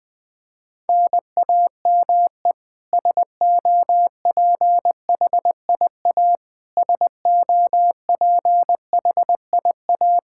Morse code, heard as a keyed tone, about 20 wpm, 700 Hz. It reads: NAME SOPHIA SOPHIA
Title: Practice QSO - Exchange 1 - Beginning - 20wpm